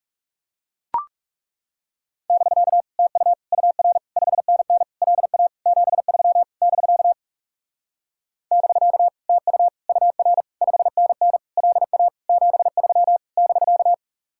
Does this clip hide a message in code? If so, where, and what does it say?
Yes: Morse code from 2.3 to 13.9 s, heard as a keyed tone, about 41 wpm, 700 Hz. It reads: <BK> TU UR 5NN LA 73 <BK> <BK> TU UR 5NN LA 73 <BK>